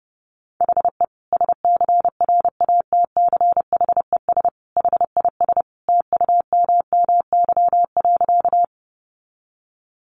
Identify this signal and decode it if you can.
Morse code, heard as a keyed tone, about 30 wpm, 700 Hz. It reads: HE SCRATCHES HIS TUMMY.
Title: Easy Phrases 30wpm